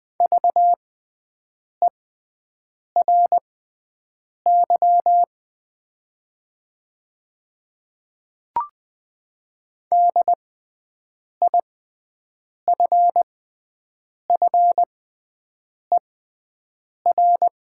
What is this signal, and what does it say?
Morse code, heard as a keyed tone, about 20 wpm, spaced as for 7 wpm, 700 Hz. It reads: VERY DIFFER
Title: 6x Character Spacing - Top 300 Words - 20wpm